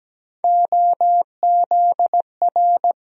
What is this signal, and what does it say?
Morse code, heard as a keyed tone, about 17 wpm, 700 Hz. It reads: OZR
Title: USA Call Sign Postfixes - 3 Letters 17wpm